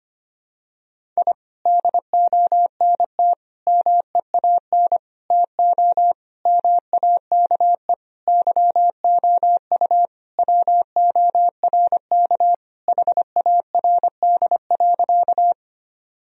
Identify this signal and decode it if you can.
Morse code, heard as a keyed tone, about 25 wpm, 700 Hz. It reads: I DONT MEAN TO MAKE YOU WORK HARD.
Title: Sentences from Top 1000 Words - Review Part 1 - 25wpm